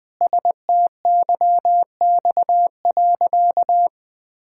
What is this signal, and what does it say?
Morse code, heard as a keyed tone, about 20 wpm, 700 Hz. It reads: STYX.
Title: Pangram Challenge v2 - 20wpm